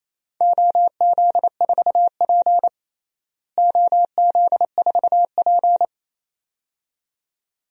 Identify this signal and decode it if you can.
Morse code, heard as a keyed tone, about 28 wpm, 700 Hz. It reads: OZ4P OZ4P
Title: DXCC-100 Call Signs - 4 Characters - Repeated 2x - 28wpm